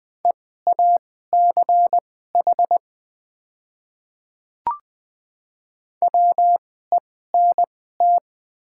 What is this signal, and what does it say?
Morse code, heard as a keyed tone, about 20 wpm, spaced as for 14 wpm, 700 Hz. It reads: EACH WENT
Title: 2x Character Spacing - Top 200 Words - 20wpm